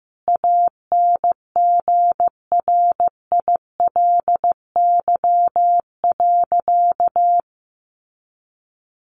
Morse code, heard as a keyed tone, about 15 wpm, 700 Hz. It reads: ANGRILY.
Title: A Sound Of Thunder 15wpm